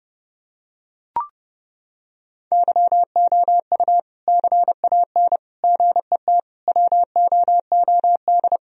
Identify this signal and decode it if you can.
Morse code, heard as a keyed tone, about 30 wpm, 700 Hz. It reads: YOU CAN GET WOOD
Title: Sentences from Top 1000 Words - Review Part 2 - 30wpm